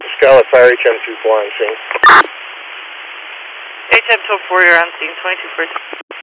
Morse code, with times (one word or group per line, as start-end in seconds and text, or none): none